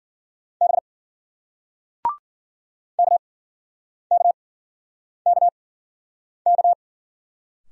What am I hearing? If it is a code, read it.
Morse code, heard as a keyed tone, about 56 wpm, spaced as for 9 wpm, 700 Hz. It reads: B KKKK